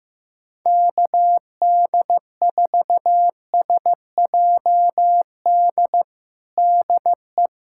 Morse code, heard as a keyed tone, about 15 wpm, 700 Hz. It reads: KD4SJD DE
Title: USA Callsign DE Callsign 15wpm